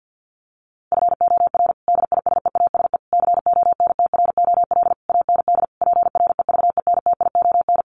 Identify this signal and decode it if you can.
Morse code, heard as a keyed tone, about 50 wpm, 700 Hz. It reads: FOR DISEASE CONTROL AND PREVENTION
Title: Sets of 5 Words Encore2 50wpm